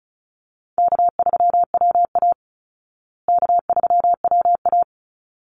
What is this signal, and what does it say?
Morse code, heard as a keyed tone, about 34 wpm, 700 Hz. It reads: K3WA K3WA
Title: Call Signs Repeated 3x - 35wpm